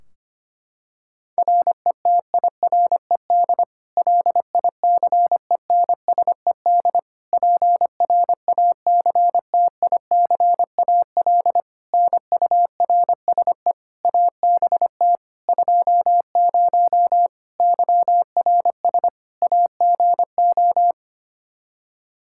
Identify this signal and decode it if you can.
Morse code, heard as a keyed tone, about 25 wpm, 700 Hz. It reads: RETIRED LICENSED PRACTICAL NURSE ABT 20 YRS AGO
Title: Practice QSO - Exchange 3 - More Info - 25wpm